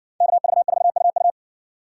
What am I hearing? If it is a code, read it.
Morse code, heard as a keyed tone, about 60 wpm, 700 Hz. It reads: KV4UU